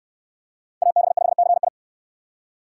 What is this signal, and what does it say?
Morse code, heard as a keyed tone, about 63 wpm, 700 Hz. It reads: AB5CI